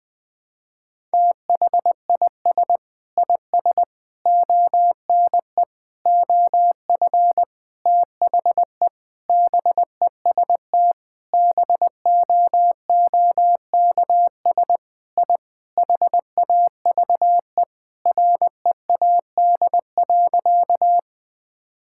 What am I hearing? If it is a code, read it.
Morse code, heard as a keyed tone, about 20 wpm, 700 Hz. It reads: THIS IS ONE OF THE BEST BOOKS I HAVE READ.
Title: Sentences with 5-Word Phrases - Spoken Beginning - 20wpm